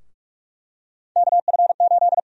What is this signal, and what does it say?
Morse code, heard as a keyed tone, about 46 wpm, 700 Hz. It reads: KF8
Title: USA Call Sign Prefixes - 45wpm